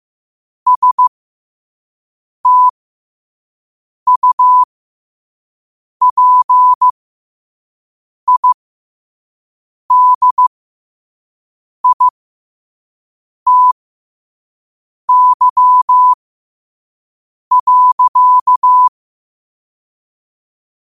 Morse code, heard as a keyed tone, about 15 wpm, spaced as for 5 wpm, 1000 Hz. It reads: STUPIDITY.
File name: qotdfiles / 5wpm